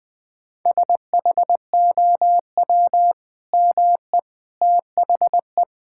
Morse code, heard as a keyed tone, about 20 wpm, 700 Hz. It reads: SHOW ME THE
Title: Sentences with 2-Word Phrases - 20wpm